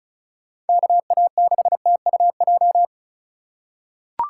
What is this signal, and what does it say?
Morse code, heard as a keyed tone, about 34 wpm, 700 Hz. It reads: KA6TUJ
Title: USA Call Signs - 6 Characters - 2x Character Spacing - 35wpm